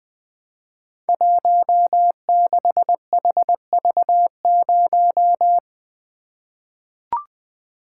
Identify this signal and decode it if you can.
Morse code, heard as a keyed tone, about 21 wpm, 700 Hz. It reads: 16HV0
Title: Code Groups - 5 Letter:Numbers - 20wpm